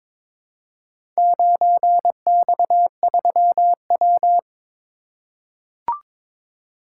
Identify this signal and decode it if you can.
Morse code, heard as a keyed tone, about 22 wpm, 700 Hz. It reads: 9X3W
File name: practice / 22wpm / Four Letter-Number Combinations - 22wpm